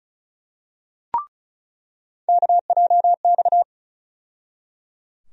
Morse code, heard as a keyed tone, about 36 wpm, 700 Hz. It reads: KJX
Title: USA Call Sign Postfixes - 3 Letters 35wpm